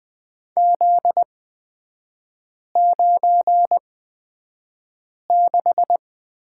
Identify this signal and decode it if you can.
Morse code, heard as a keyed tone, about 20 wpm, spaced as for 5 wpm, 700 Hz. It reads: Z96
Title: 243 Lesson 39e - Letter z - Three Characters 20wpm